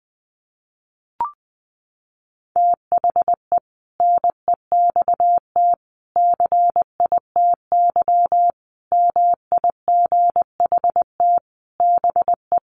Morse code, heard as a keyed tone, about 20 wpm, 700 Hz. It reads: THE NEXT CITY MIGHT BE
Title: Sentences from Top 1400 Words 20wpm